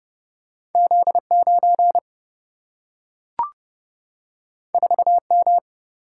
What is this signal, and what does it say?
Morse code, heard as a keyed tone, about 30 wpm, 700 Hz. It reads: Z9 4M
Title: Single-Letter Single-Number 30wpm